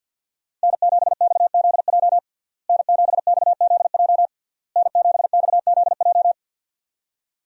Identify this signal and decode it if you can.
Morse code, heard as a keyed tone, about 52 wpm, 700 Hz. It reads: N7XZJ N7XZJ N7XZJ
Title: USA Call Signs - 5 Characters - Repeated 3x - 50wpm